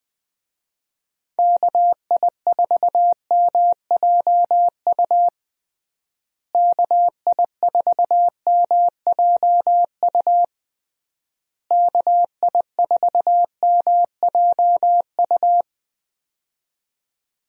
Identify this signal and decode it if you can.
Morse code, heard as a keyed tone, about 20 wpm, 700 Hz. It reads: KI4MJU KI4MJU KI4MJU
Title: USA Call Signs - 6 Characters - Repeated 3x - 20wpm